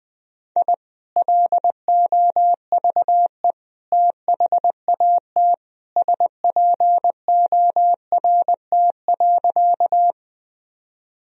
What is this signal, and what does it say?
Morse code, heard as a keyed tone, about 20 wpm, 700 Hz. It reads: I LOVE THAT SPORT.
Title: Sentences from Top 1000 Words - Review Part 4 - 20wpm